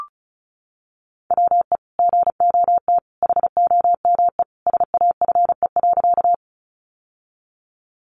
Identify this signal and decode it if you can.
Morse code, heard as a keyed tone, about 34 wpm, 700 Hz. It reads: WE GOT HOME SAFE.
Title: Sentences from Top 900 Words 35wpm